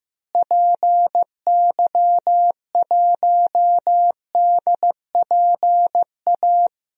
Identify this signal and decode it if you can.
Morse code, heard as a keyed tone, about 15 wpm, 700 Hz. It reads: PY1DPA
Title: DXCC-100 Call Signs - 6 Characters - Repeated 2x - 15wpm